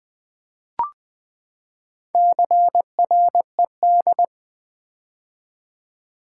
Morse code, heard as a keyed tone, about 20 wpm, 700 Hz. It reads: CRED
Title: Build A Word - 4 Letters At A Time - 8-Letter Words - 20wpm